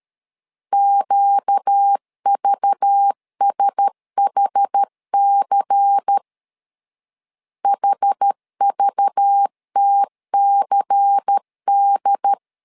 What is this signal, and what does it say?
Morse code, heard as a keyed tone, about 13 wpm, 800 Hz. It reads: QVSHC HVTCD
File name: LEZIONI / 5 LBVC / 5 LBVC 12wpm 1955